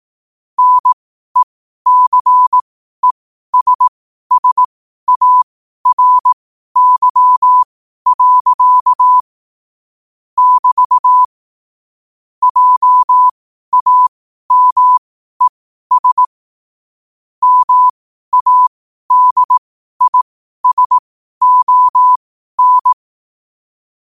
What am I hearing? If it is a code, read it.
Morse code, heard as a keyed tone, about 18 wpm, spaced as for 12 wpm, 1000 Hz. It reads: NECESSARY. <BT> JAMES MADISON
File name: qotdfiles / 12wpm